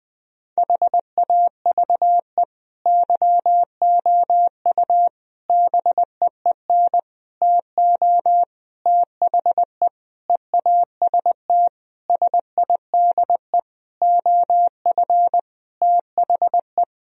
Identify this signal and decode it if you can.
Morse code, heard as a keyed tone, about 20 wpm, 700 Hz. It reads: HAVE YOU BEEN TO THE EAST SIDE OF THE
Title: Sentences from Top 1000 Words - Review Part 3 - 20wpm